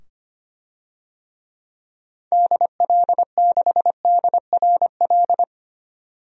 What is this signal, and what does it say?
Morse code, heard as a keyed tone, about 25 wpm, 700 Hz. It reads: DL6DRL
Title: DXCC-100 Call Signs - 6 Characters - 25wpm